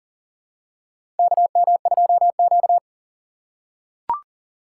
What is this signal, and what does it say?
Morse code, heard as a keyed tone, about 40 wpm, 700 Hz. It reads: KM2Q
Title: USA Call Signs - 6x Character Spacing - 40wpm